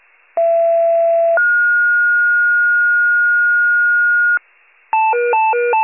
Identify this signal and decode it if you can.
Two-tone page, heard as a one-second tone, then a three-second tone, about 670 then 1430 Hz. It is heard from 0.4 to 4.4 s.
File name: wcalert